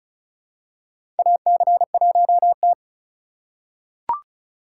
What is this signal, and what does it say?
Morse code, heard as a keyed tone, about 36 wpm, 700 Hz. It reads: AC1T
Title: USA Call Signs - 4 Characters - 4x Character Spacing - 35wpm